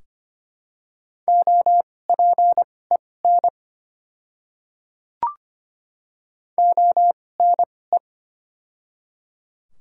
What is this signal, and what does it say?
Morse code, heard as a keyed tone, about 25 wpm, spaced as for 18 wpm, 700 Hz. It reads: OPEN ONE
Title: 2x Character Spacing - Top 500 Words - 25wpm